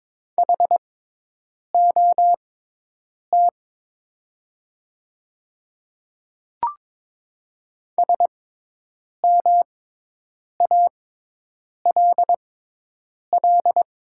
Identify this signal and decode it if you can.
Morse code, heard as a keyed tone, about 22 wpm, spaced as for 8 wpm, 700 Hz. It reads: HOT SMALL
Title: 6x Character Spacing - Top 200 Words - 22wpm